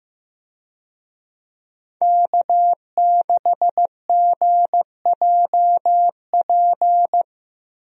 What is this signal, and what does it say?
Morse code, heard as a keyed tone, about 15 wpm, 700 Hz. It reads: K6GJP
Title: USA Call Signs - 5 Characters - Repeated 2x - 15wpm